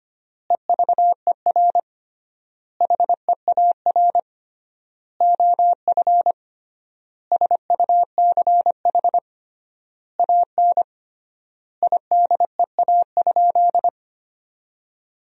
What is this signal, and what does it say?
Morse code, heard as a keyed tone, about 25 wpm, 700 Hz. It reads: EVER HEAR OF SUCH AN IDEA?
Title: 3x Word-Spacing - Sentences from Top 400 Words - 25wpm